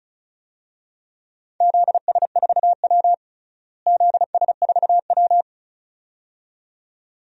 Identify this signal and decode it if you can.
Morse code, heard as a keyed tone, about 34 wpm, 700 Hz. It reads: ZS4W ZS4W
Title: DXCC-100 Call Signs - 4 Characters - Repeated 2x - 35wpm